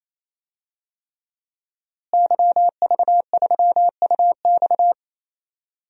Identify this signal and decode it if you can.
Morse code, heard as a keyed tone, about 28 wpm, 700 Hz. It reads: YV3UX
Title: DXCC-100 Call Signs - 5 Characters - Repeated 3x - 28wpm